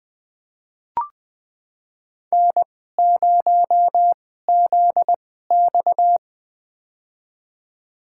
Morse code, heard as a keyed tone, about 20 wpm, spaced as for 14 wpm, 700 Hz. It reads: N0ZX
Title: USA Call Signs - 4 Characters - 2x Character Spacing - 20wpm